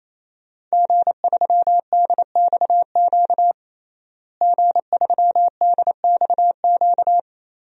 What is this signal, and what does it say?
Morse code, heard as a keyed tone, about 28 wpm, 700 Hz. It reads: G3DXQ G3DXQ
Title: DXCC-100 Call Signs - 5 Characters - Repeated 3x - 28wpm